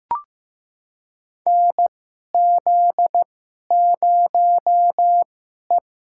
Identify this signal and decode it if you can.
Morse code, heard as a keyed tone, about 15 wpm, spaced as for 11 wpm, 700 Hz. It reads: NZ0E
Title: USA Call Signs - 4 Characters - 2x Character Spacing - 15wpm